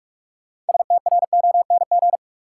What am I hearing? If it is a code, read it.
Morse code, heard as a keyed tone, about 44 wpm, 700 Hz. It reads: STRONG
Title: Sentences from Top 500 Words - New Word Drill - 45wpm